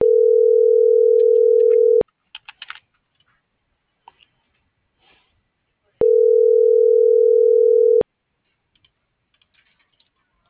Ringback tone (North America), 440 + 480 Hz.
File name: CENTRAL MABE / out luis.palacios - .31677